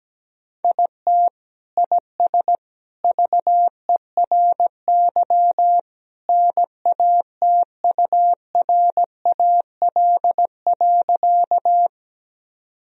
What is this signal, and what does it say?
Morse code, heard as a keyed tone, about 17 wpm, 700 Hz. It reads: IT IS VERY NATURAL.
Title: Sentences from Top 1000 Words - Review Part 4 - 17wpm